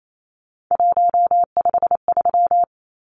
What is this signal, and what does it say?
Morse code, heard as a keyed tone, about 28 wpm, 700 Hz. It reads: 153
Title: Numbers - 3-Digit 28wpm